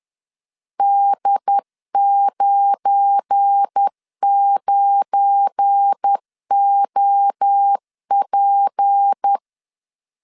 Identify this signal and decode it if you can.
Morse code, heard as a keyed tone, about 10 wpm, 800 Hz. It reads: D99OP
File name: LEZIONI / 7 GF9Z / 7 GF9 10wpm 2747